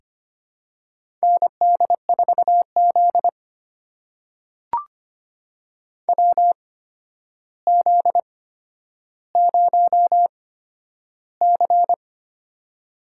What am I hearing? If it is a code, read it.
Morse code, heard as a keyed tone, about 25 wpm, spaced as for 11 wpm, 700 Hz. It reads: ND4Z WZ0C
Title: USA Call Signs - 4 Characters - 8x Character Spacing - 25wpm